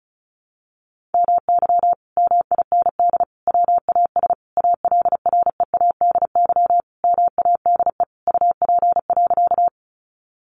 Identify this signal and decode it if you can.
Morse code, heard as a keyed tone, about 34 wpm, 700 Hz. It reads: MY MIND WAS ALREADY MADE UP.
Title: Sentences from Top 700 Words 35wpm